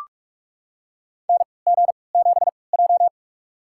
Morse code, heard as a keyed tone, about 42 wpm, spaced as for 25 wpm, 700 Hz. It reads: NG7J
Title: USA Call Signs - 4 Characters - 3x Character Spacing - 45wpm